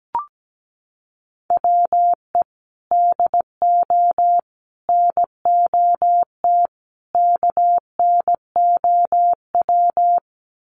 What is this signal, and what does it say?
Morse code, heard as a keyed tone, about 17 wpm, 700 Hz. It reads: WE DO NOT KNOW
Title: Sets of 4 Words 17wpm